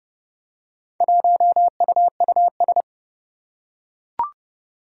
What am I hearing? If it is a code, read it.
Morse code, heard as a keyed tone, about 30 wpm, 700 Hz. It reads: 1UUS
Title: Four Challenging Letter-Number Combinations 30wpm